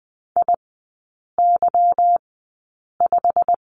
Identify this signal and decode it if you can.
Morse code, heard as a keyed tone, about 20 wpm, spaced as for 8 wpm, 700 Hz. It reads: IY5
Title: 145 Lesson 24e - Letter y - Three Characters 20wpm